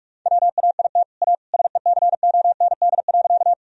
Morse code, heard as a keyed tone, about 46 wpm, 700 Hz. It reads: WAIT A SECOND.
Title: Sentences from Top 1000 Words - Review Part 1 - 45wpm